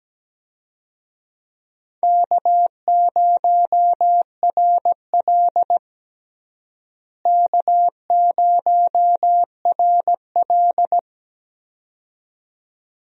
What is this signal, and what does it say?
Morse code, heard as a keyed tone, about 17 wpm, 700 Hz. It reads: K0RL K0RL
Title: USA Call Signs - 4 Characters - Repeated 2x - 17wpm